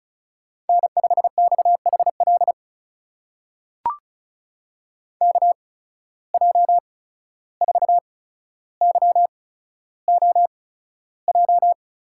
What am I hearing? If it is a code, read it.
Morse code, heard as a keyed tone, about 36 wpm, spaced as for 14 wpm, 700 Hz. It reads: N5XHL KJ4YOJ
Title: USA Call Signs - 8x Character Spacing - 35wpm